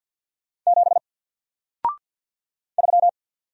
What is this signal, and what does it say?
Morse code, heard as a keyed tone, about 50 wpm, 700 Hz. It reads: 7 3
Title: Single Character (Letter:Number:Punctuation:Prosign) 50wpm